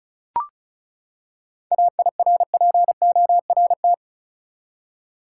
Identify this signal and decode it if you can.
Morse code, heard as a keyed tone, about 34 wpm, 700 Hz. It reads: AIRPORT
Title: Seven Letter Words - 35wpm